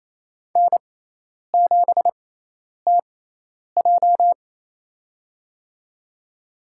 Morse code, heard as a keyed tone, about 27 wpm, spaced as for 10 wpm, 700 Hz. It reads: N7TJ
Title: USA Call Signs - 4 Characters - 6x Character Spacing - 28wpm